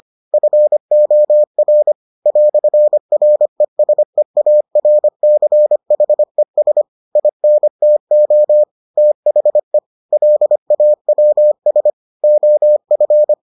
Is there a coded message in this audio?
Yes: Morse code from 0.3 to 13.3 s, heard as a keyed tone, about 25 wpm, 600 Hz. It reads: FOR "RESEARCHES INTO THE LAWS OF